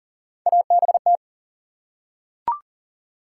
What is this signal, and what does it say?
Morse code, heard as a keyed tone, about 40 wpm, 700 Hz. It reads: ABT
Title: Top 90 QSO Elements 40wpm